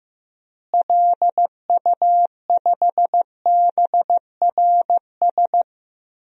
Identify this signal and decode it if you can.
Morse code, heard as a keyed tone, about 15 wpm, 700 Hz. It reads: LU5BRS
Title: DXCC-100 Call Signs - 6 Characters - Repeated 3x - 15wpm